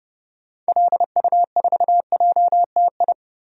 Morse code, heard as a keyed tone, about 30 wpm, 700 Hz. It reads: LU4JTI
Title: DXCC-100 Call Signs - 6 Characters - 30wpm